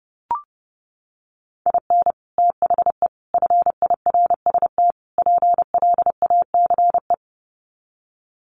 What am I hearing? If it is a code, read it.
Morse code, heard as a keyed tone, about 28 wpm, 700 Hz. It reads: IN THE FIRST PLACE
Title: Sets of 4 Words 30wpm